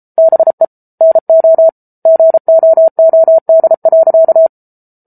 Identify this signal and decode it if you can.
Morse code, heard as a keyed tone, about 34 wpm, 650 Hz. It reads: BE NO GOOD.